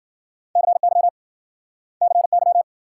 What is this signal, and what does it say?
Morse code, heard as a keyed tone, about 53 wpm, 700 Hz. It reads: CY CY